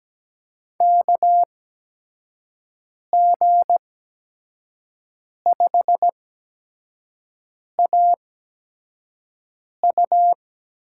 Morse code, heard as a keyed tone, about 17 wpm, spaced as for 5 wpm, 700 Hz. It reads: KG5AU
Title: USA Call Signs - 5 Characters - 8x Character Spacing - 17wpm